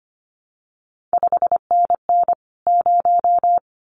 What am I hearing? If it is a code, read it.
Morse code, heard as a keyed tone, about 25 wpm, 700 Hz. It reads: 5NN 0
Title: CQ WW DX Contest Exchange Encore - 25wpm